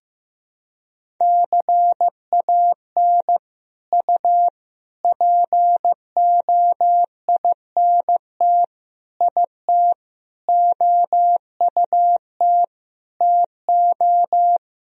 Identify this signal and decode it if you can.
Morse code, heard as a keyed tone, about 15 wpm, 700 Hz. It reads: CAN U POINT IT OUT TO
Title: Sentences from Top 200 Words and Abbreviations 15wpm